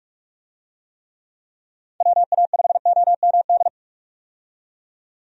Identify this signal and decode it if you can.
Morse code, heard as a keyed tone, about 46 wpm, 700 Hz. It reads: WA5OMD
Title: USA Call Signs - 6 Characters - 45wpm